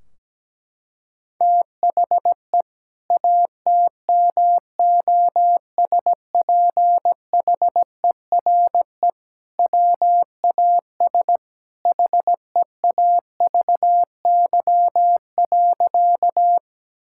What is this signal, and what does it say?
Morse code, heard as a keyed tone, about 17 wpm, 700 Hz. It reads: THE ATMOSPHERE WAS HEAVY.
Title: Sentences from Top 1600 Words 17wpm